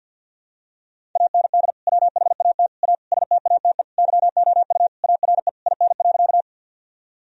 Morse code, heard as a keyed tone, about 50 wpm, 700 Hz. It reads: AND WHAT A STATE YOU ARE IN.
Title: Sentences from Top 1000 Words - Review Part 2 - 50wpm